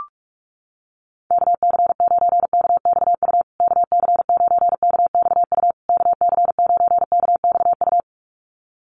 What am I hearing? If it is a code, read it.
Morse code, heard as a keyed tone, about 44 wpm, 700 Hz. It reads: KC9KXU KC9KXU KC9KXU
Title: USA Call Signs Repeated 3x 45wpm